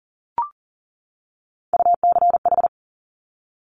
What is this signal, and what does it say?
Morse code, heard as a keyed tone, about 38 wpm, 700 Hz. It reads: UCH
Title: Top Three Letter Combinations in Words 40wpm